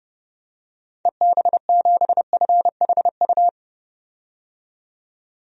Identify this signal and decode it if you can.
Morse code, heard as a keyed tone, about 32 wpm, 700 Hz. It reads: EB7FHU
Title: DXCC-100 Call Signs - 6 Characters - 30wpm